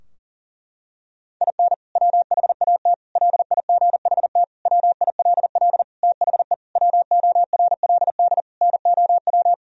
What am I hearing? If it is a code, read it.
Morse code, heard as a keyed tone, about 40 wpm, 700 Hz. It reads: IN WHAT LIGHT WILL THE WORLD NOW
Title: Sentences from Top 1000 Words - Review Part 2 - 40wpm